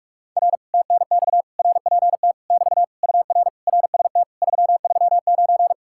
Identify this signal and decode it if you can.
Morse code, heard as a keyed tone, about 49 wpm, 700 Hz. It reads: R TNX RPT <BT> UR RST 339